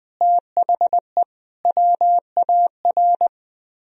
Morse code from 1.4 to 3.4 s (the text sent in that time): WAR